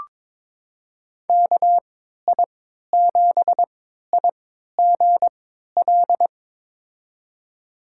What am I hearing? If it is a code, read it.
Morse code, heard as a keyed tone, about 22 wpm, spaced as for 13 wpm, 700 Hz. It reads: KI7IGL
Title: USA Call Signs - 3x Character Spacing - 22wpm